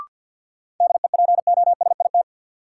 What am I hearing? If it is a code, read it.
Morse code, heard as a keyed tone, about 52 wpm, 700 Hz. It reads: DEPOSIT
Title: Sentences from Top 1300 Words - New Word Drill Alt - 50wpm